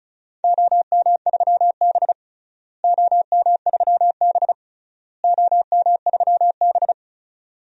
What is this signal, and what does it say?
Morse code, heard as a keyed tone, about 36 wpm, 700 Hz. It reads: OM3B OM3B OM3B
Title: DXCC-100 Call Signs - 4 Characters - Repeated 3x - 35wpm